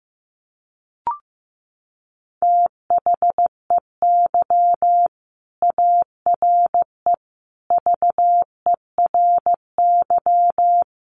Morse code, heard as a keyed tone, about 15 wpm, 700 Hz. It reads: THEY ARE VERY
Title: Sentences from Top 700 Words 15wpm